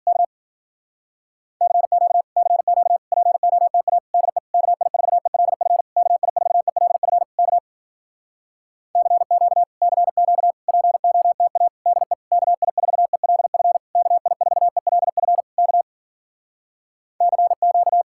Morse code, heard as a keyed tone, about 52 wpm, 700 Hz. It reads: K CQ CQ POTA DE KI4ELF KI4ELF K CQ CQ POTA DE KI4ELF KI4ELF K CQ